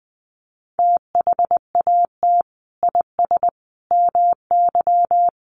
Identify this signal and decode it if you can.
Morse code, heard as a keyed tone, about 20 wpm, 700 Hz. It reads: THAT IS MY
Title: Sentences from Top 1100 Words 20wpm